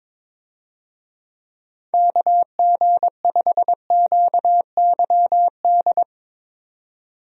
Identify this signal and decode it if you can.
Morse code, heard as a keyed tone, about 22 wpm, 700 Hz. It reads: KG5QYD
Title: USA Call Signs - 6 Characters - 22wpm